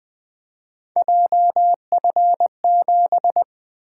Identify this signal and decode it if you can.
Morse code, heard as a keyed tone, about 20 wpm, 700 Hz. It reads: JF7
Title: Three Letter-Number Combinations - 20wpm